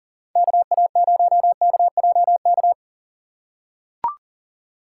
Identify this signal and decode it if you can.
Morse code, heard as a keyed tone, about 40 wpm, 700 Hz. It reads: KA0KJK
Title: USA Call Signs - 6 Characters - 6x Character Spacing - 40wpm